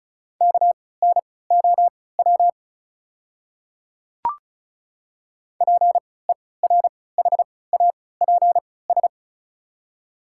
Morse code, heard as a keyed tone, about 34 wpm, spaced as for 20 wpm, 690 Hz. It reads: KNOW PERHAPS
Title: 3x Character Spacing - Top 500 Words - 35wpm